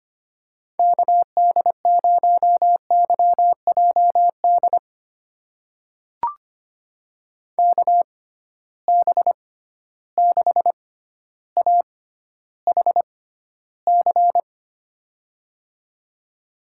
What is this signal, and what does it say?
Morse code, heard as a keyed tone, about 25 wpm, spaced as for 13 wpm, 700 Hz. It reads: KD0YJD KB6AHC
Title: USA Call Signs - 6 Characters - 6x Character Spacing - 25wpm